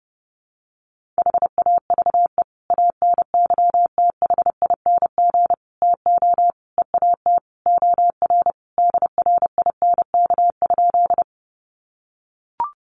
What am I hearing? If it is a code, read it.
Morse code, heard as a keyed tone, about 30 wpm, 700 Hz. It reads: HAVE ANYTHING TO EAT OR DRINK?
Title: Sentences from Top 1500 Words 30wpm